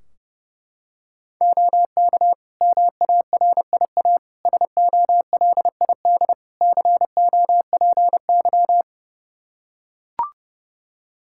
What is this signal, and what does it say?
Morse code, heard as a keyed tone, about 32 wpm, 700 Hz. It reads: OK MARIA SOLID COPY
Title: Practice QSO - Exchange 3 - More Info - 30wpm